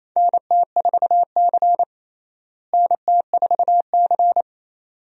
Morse code, heard as a keyed tone, about 28 wpm, 700 Hz. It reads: NT4C NT4C
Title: USA Call Signs - 4 Characters - Repeated 2x - 28wpm